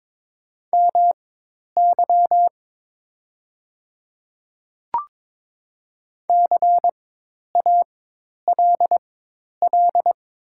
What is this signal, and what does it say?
Morse code, heard as a keyed tone, about 22 wpm, spaced as for 10 wpm, 700 Hz. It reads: MY CALL
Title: 4x Character Spacing - Top 100 Words - 22wpm